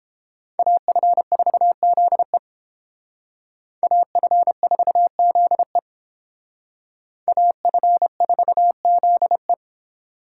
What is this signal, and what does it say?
Morse code, heard as a keyed tone, about 30 wpm, 700 Hz. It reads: AF4ZE AF4ZE AF4ZE